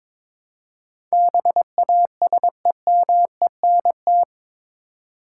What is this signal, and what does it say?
Morse code, heard as a keyed tone, about 22 wpm, 700 Hz. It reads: BASEMENT
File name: practice / 22wpm / Eight Letter Words - 22wpm